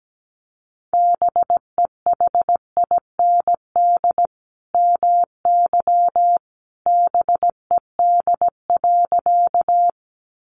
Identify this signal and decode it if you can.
Morse code, heard as a keyed tone, about 17 wpm, 700 Hz. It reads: BEHIND MY BED.
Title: Sentences from Top 500 Words 17wpm